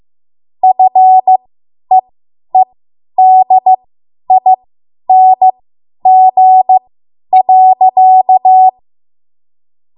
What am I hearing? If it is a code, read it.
Morse code, heard as a keyed tone, about 15 wpm, spaced as for 10 wpm, 750 Hz. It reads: FEEDING.
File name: Morse / 10WPM